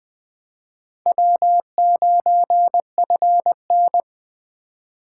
Morse code, heard as a keyed tone, about 22 wpm, 700 Hz. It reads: W9FN